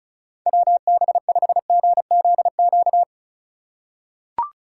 Morse code, heard as a keyed tone, about 36 wpm, 700 Hz. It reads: WB5GZQ
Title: USA Call Signs - 6 Characters - 4x Character Spacing - 35wpm